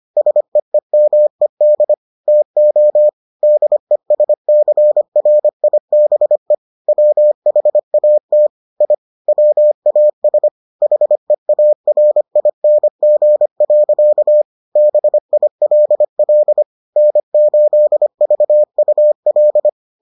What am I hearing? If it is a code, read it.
Morse code, heard as a keyed tone, about 25 wpm, 600 Hz. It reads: SEEMED TO DESCRIBE WHAT I WAS HEARING. BILL N8VUL